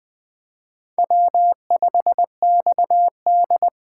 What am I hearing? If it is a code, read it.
Morse code, heard as a keyed tone, about 20 wpm, 700 Hz. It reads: W5XD
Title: Speed Racing - USA Call Signs 17wpm